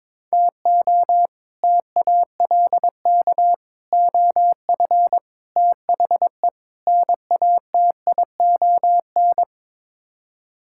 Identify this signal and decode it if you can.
Morse code, heard as a keyed tone, about 22 wpm, 700 Hz. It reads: TO TALK OF THE NATION
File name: practice / 22wpm / Build A Phrase - 5-Word Phrases - 22wpm